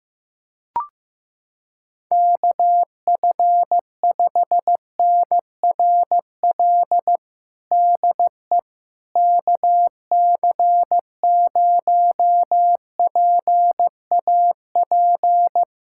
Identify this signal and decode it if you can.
Morse code, heard as a keyed tone, about 15 wpm, 700 Hz. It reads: KF5NRL DE KC0PAP
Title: USA Callsign DE Callsign 15wpm